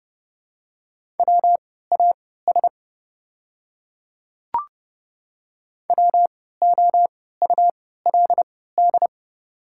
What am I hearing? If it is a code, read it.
Morse code, heard as a keyed tone, about 30 wpm, spaced as for 17 wpm, 700 Hz. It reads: WAS WOULD
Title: 3x Character Spacing - Top 100 Words - 30wpm